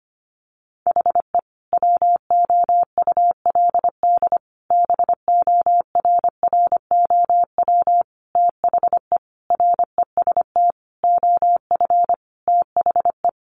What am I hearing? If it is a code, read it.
Morse code, heard as a keyed tone, about 25 wpm, 700 Hz. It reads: HE WOULD BORROW THE REST OF THE
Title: Sentences from Top 1500 Words 25wpm